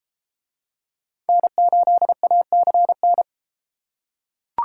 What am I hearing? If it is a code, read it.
Morse code, heard as a keyed tone, about 32 wpm, 700 Hz. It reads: N8ACN